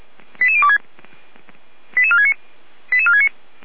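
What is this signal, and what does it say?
Five-tone selective call (ZVEI), heard as quick runs of five tones, tones of about 70 ms. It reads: 88026 88469 88469
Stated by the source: ZVEI five-tone selective call, code 88102, received on an SDR.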